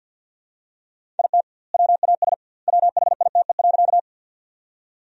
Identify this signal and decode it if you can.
Morse code, heard as a keyed tone, about 47 wpm, 700 Hz. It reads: IT WAS WHITE.
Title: 2x Word-Spacing - Sentences from Top 300 Words - 50wpm